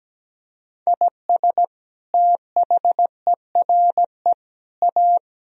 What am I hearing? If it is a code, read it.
Morse code, heard as a keyed tone, about 17 wpm, 700 Hz. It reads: IS THERE A
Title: Sentences from Top 1000 Words - Review Part 4 - 17wpm